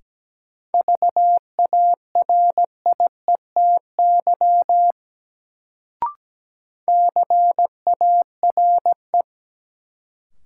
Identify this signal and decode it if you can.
Morse code, heard as a keyed tone, about 17 wpm, 700 Hz. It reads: VARIETY CARE
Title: Sets of 3 Words - New Word Drill - 17wpm